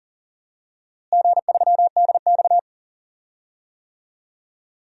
Morse code, heard as a keyed tone, about 39 wpm, 700 Hz. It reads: G3DX
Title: DXCC-100 Call Signs - 4 Characters - Repeated 2x - 40wpm